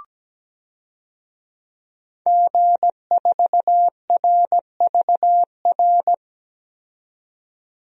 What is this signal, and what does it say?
Morse code, heard as a keyed tone, about 17 wpm, 700 Hz. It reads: G4RVR
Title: DXCC-100 Call Signs - 5 Characters - 17wpm